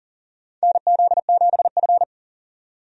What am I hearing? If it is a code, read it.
Morse code, heard as a keyed tone, about 40 wpm, 700 Hz. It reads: NZ7F
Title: USA Call Signs - 4 Characters - 40wpm